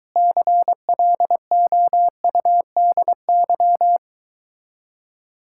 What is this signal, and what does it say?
Morse code, heard as a keyed tone, about 23 wpm, 700 Hz. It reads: CLOUDY